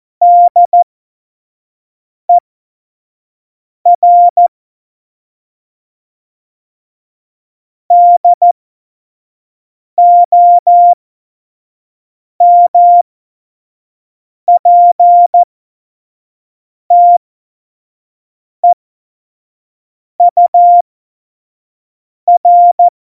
Morse code, heard as a keyed tone, about 14 wpm, spaced as for 5 wpm, 700 Hz. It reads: DER DOMPTEUR
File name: Morsekurs / Witze / Witz 14. August 2018, 25 F70